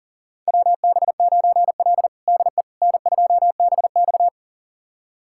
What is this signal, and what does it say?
Morse code, heard as a keyed tone, about 40 wpm, 700 Hz. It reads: WB9L DE N2BX
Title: USA Callsign DE Callsign - 4 Characters - 40wpm